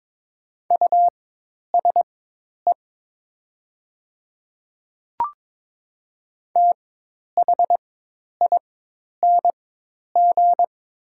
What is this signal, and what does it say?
Morse code, heard as a keyed tone, about 23 wpm, spaced as for 10 wpm, 700 Hz. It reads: USE THING
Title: 4x Character Spacing - Top 100 Words - 22wpm